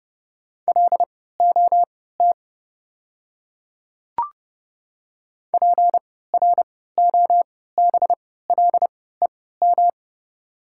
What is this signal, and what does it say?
Morse code, heard as a keyed tone, about 30 wpm, spaced as for 17 wpm, 700 Hz. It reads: LOT PROBLEM
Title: 3x Character Spacing - Top 500 Words - 30wpm